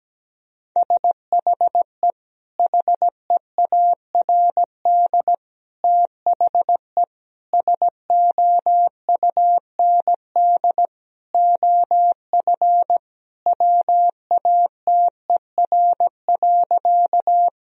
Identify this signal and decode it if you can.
Morse code, heard as a keyed tone, about 17 wpm, 700 Hz. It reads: SHE HEARD THE SOUND OF WATER.
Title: Sentences from Top 1000 Words - Review Part 1 - 17wpm